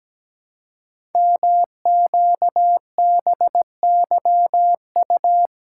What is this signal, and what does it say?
Morse code, heard as a keyed tone, about 17 wpm, 700 Hz. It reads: MQBYU
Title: Code Groups - 5 Letters - 17wpm